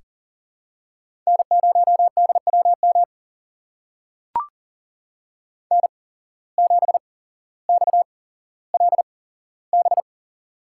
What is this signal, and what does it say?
Morse code, heard as a keyed tone, about 37 wpm, spaced as for 17 wpm, 700 Hz. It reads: N0DWM N7XLB
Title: USA Call Signs - 5 Characters - 8x Character Spacing - 40wpm